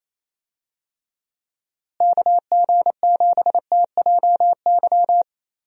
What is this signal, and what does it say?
Morse code, heard as a keyed tone, about 28 wpm, 700 Hz. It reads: KG7TJY
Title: USA Call Signs - 6 Characters - Repeated 3x - 28wpm